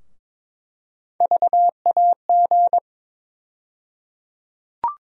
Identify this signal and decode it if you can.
Morse code, heard as a keyed tone, about 22 wpm, 700 Hz. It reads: VAG
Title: USA Call Sign Postfixes - 3 Letters 22wpm